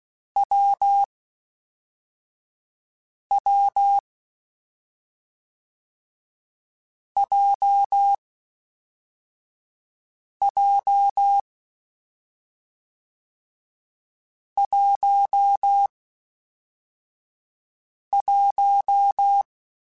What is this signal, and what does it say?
Morse code, heard as a keyed tone, about 16 wpm, spaced as for 4 wpm, 800 Hz. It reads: WW JJ 11
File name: Difficult Combos / Difficult Combos-mp3s / AWJ1 @5